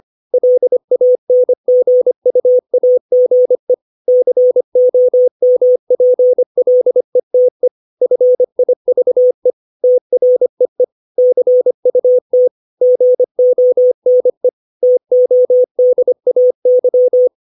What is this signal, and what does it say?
Morse code, heard as a keyed tone, about 25 wpm, 500 Hz. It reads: LANGUAGE COMPLETE FIVE TREE CUT GONE TODAY
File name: 632 words / 632 Words 25WPM